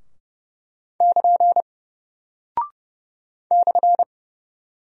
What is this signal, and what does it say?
Morse code, heard as a keyed tone, about 30 wpm, 700 Hz. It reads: <KN> /
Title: Single Character (Letter:Number:Punctuation:Prosign) Alt 30wpm